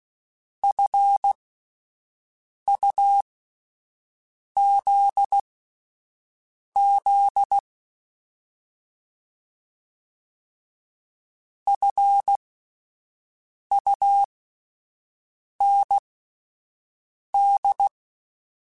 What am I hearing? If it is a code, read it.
Morse code, heard as a keyed tone, about 16 wpm, spaced as for 5 wpm, 800 Hz. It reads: FUZZ FUND
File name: Alphabetical / Alphabetical mp3s / Fs @5